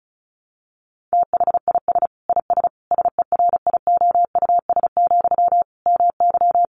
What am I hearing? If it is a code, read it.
Morse code, heard as a keyed tone, about 34 wpm, 700 Hz. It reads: THIS IS SERIOUS, MY
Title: Sentences from Top 1100 Words 35wpm